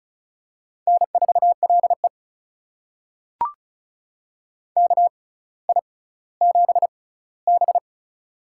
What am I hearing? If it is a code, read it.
Morse code, heard as a keyed tone, about 36 wpm, spaced as for 18 wpm, 700 Hz. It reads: N4LE KI7B
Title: USA Call Signs - 4 Characters - 6x Character Spacing - 35wpm